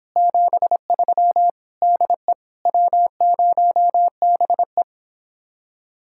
Morse code, heard as a keyed tone, about 26 wpm, 700 Hz. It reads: 73 DE W0BE